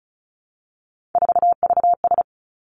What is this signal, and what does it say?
Morse code, heard as a keyed tone, about 36 wpm, 700 Hz. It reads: 4VS
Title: Three Challenging Letter-Number Combinations 35wpm